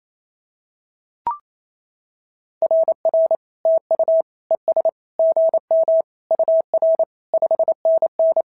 Morse code, heard as a keyed tone, about 27 wpm, 650 Hz. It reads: RR TU ES GM UR 5NN
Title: SOTA QSO Practice 28wpm